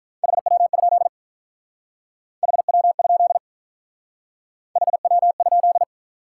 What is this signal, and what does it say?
Morse code, heard as a keyed tone, about 43 wpm, 700 Hz. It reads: HW? HW? HW?
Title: Speed-Racing Top 180 QSO Elements 35wpm